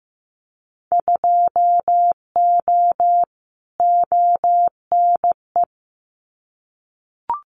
Morse code, heard as a keyed tone, about 15 wpm, 700 Hz. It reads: 2O ONE
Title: Winter Field Day Exchanges Encore 15wpm